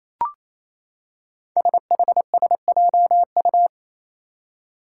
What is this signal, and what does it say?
Morse code, heard as a keyed tone, about 28 wpm, 700 Hz. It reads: SHSJU
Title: Five Challenging Letter-Number Combinations 28wpm